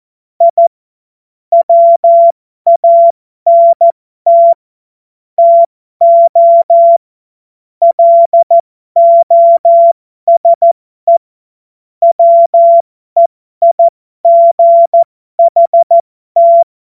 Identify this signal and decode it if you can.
Morse code, heard as a keyed tone, about 14 wpm, spaced as for 12 wpm, 680 Hz. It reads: I WANT TO LOSE WEIGHT
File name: Morsekurs / Witze